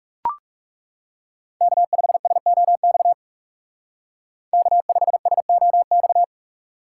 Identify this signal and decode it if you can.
Morse code, heard as a keyed tone, about 37 wpm, 700 Hz. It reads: K5SOX K5SOX
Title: Speed Racing - USA 1x3 Call Signs 30wpm